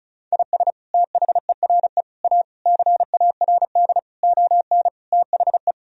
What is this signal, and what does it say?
Morse code, heard as a keyed tone, about 36 wpm, 700 Hz. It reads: IS THERE A CARD ON THE
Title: Sentences from Top 1000 Words - Review Part 4 - 35wpm